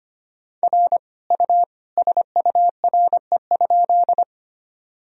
Morse code, heard as a keyed tone, about 25 wpm, 700 Hz. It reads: R U SURE?
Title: Sentences from Top 300 Words and Abbreviations 25wpm